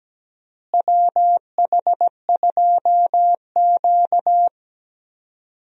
Morse code, heard as a keyed tone, about 17 wpm, 700 Hz. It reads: WH2Q
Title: USA Call Signs - 4 Characters - Repeated 3x - 17wpm